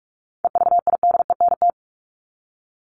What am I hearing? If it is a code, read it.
Morse code, heard as a keyed tone, about 44 wpm, 700 Hz. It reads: EVIDENT
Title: LY Suffix - Top 100 Words with Base Words 45wpm